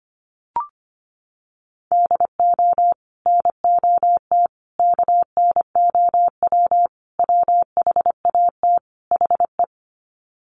Morse code, heard as a keyed tone, about 25 wpm, 700 Hz. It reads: DO NOT KNOW WHAT HE
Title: Sets of 5 Words Encore2 25wpm